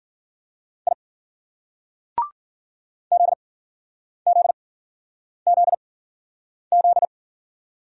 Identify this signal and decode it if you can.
Morse code, heard as a keyed tone, about 54 wpm, 700 Hz. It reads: I Z Z Z Z